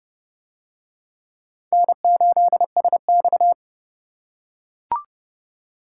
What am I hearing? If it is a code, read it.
Morse code, heard as a keyed tone, about 27 wpm, 700 Hz. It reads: N8SX